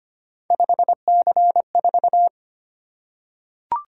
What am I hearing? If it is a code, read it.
Morse code, heard as a keyed tone, about 25 wpm, 700 Hz. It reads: 5C4
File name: Three Letter-Number Combinations - 25wpm